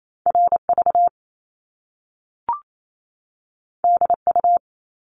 Morse code, heard as a keyed tone, about 28 wpm, 700 Hz. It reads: RV DU
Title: Top Two Letter Combinations in Words 28wpm